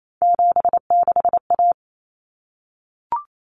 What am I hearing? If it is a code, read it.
Morse code, heard as a keyed tone, about 28 wpm, 700 Hz. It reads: 76A
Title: Common Cut Numbers - 3-Digit 28wpm